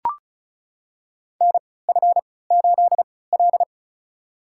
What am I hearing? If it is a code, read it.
Morse code, heard as a keyed tone, about 36 wpm, spaced as for 20 wpm, 700 Hz. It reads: NF8L
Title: USA Call Signs - 4 Characters - 3x Character Spacing - 35wpm